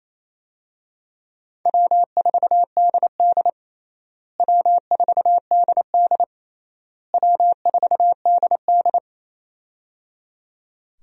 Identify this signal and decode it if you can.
Morse code, heard as a keyed tone, about 28 wpm, 700 Hz. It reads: W4DD W4DD W4DD
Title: USA Call Signs - 4 Characters - Repeated 3x - 28wpm